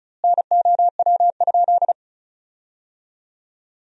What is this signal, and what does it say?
Morse code, heard as a keyed tone, about 35 wpm, 700 Hz. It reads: NOW?
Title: 2x Word-Spacing - Sentences from Top 400 Words - 35wpm